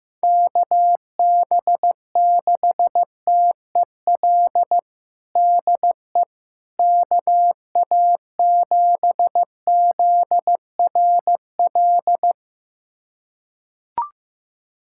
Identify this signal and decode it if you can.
Morse code, heard as a keyed tone, about 15 wpm, 700 Hz. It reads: KB6TEL DE KA7ZRL
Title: USA Callsign DE Callsign 15wpm